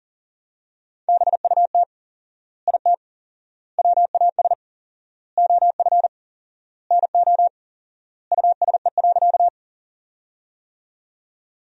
Morse code, heard as a keyed tone, about 40 wpm, 700 Hz. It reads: BUT IT WAS OF NO USE.
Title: 4x Word-Spacing - Sentences from Top 100 Words - 40wpm